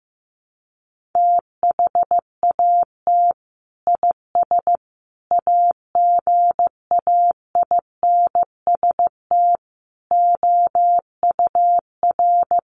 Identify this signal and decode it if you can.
Morse code, heard as a keyed tone, about 15 wpm, 700 Hz. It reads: THAT IS AGAINST OUR
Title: Sentences from Top 1500 Words 15wpm